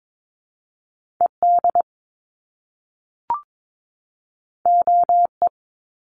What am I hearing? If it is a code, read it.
Morse code, heard as a keyed tone, about 22 wpm, 700 Hz. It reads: ED OE
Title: Top Two Letter Combinations in Words 22wpm